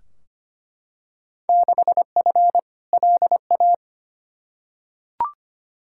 Morse code, heard as a keyed tone, about 25 wpm, 700 Hz. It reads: 6F LA
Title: Field Day Exchanges Encore - 25wpm